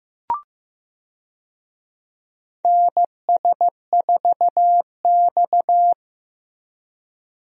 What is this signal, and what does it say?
Morse code, heard as a keyed tone, about 15 wpm, 700 Hz. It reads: NS4X
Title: USA Call Signs - 4 Characters - Repeated 3x - 15wpm